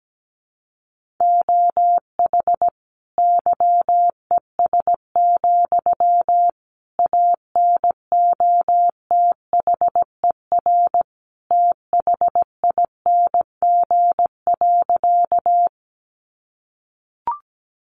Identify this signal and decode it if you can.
Morse code, heard as a keyed tone, about 17 wpm, 700 Hz. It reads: OH YES, ANOTHER THING.
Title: Sentences from Top 600 Words 17wpm